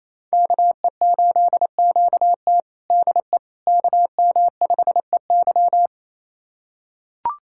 Morse code, heard as a keyed tone, about 28 wpm, 700 Hz. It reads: KE8QT DE KM5EY
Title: USA Callsign DE Callsign - 5 Characters - 28wpm